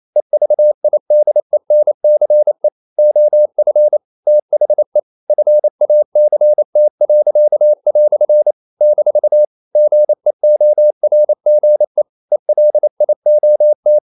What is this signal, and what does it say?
Morse code, heard as a keyed tone, about 28 wpm, 600 Hz. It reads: EVIDENCE OF THE FACT." - GEORGE ELIOT
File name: CW / Practice / EnglishQuotations0228